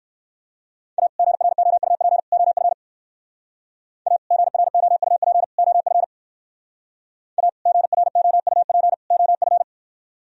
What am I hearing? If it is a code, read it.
Morse code, heard as a keyed tone, about 56 wpm, 700 Hz. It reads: A GROUP OF A GROUP OF A GROUP OF
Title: Speed-Racing - Sets of 3 Words 45wpm